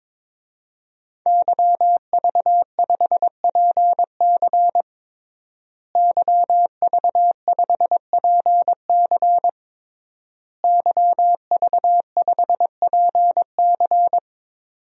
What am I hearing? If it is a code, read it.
Morse code, heard as a keyed tone, about 22 wpm, 700 Hz. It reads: YV5PC YV5PC YV5PC
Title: DXCC-100 Call Signs - 5 Characters - Repeated 3x - 22wpm